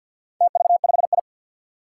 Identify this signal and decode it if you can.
Morse code, heard as a keyed tone, about 51 wpm, 700 Hz. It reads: TV5I
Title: Four Challenging Letter-Number Combinations 50wpm